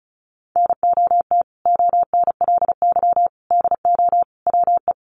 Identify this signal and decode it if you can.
Morse code, heard as a keyed tone, about 34 wpm, 700 Hz. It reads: NOT ONLY DO WE
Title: Sentences from Top 1200 Words 35wpm